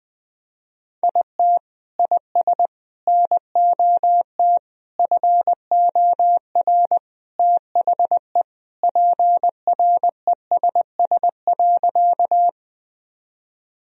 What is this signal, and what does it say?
Morse code, heard as a keyed tone, about 21 wpm, 700 Hz. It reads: IT IS NOT FOR THE PRESS.
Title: Sentences from Top 1000 Words - Review Part 1 - 20wpm